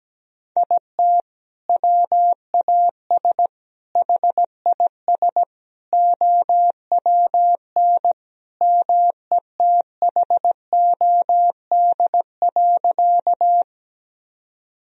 Morse code, heard as a keyed tone, about 17 wpm, 700 Hz. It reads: IT WAS HIS OWN METHOD.
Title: Sentences from Top 1000 Words - Review Part 3 - 17wpm